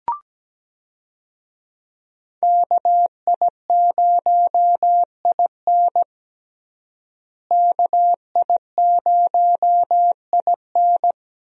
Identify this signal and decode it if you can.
Morse code, heard as a keyed tone, about 17 wpm, 700 Hz. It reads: KI0IN KI0IN
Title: USA Call Signs - 5 Characters - Repeated 2x - 17wpm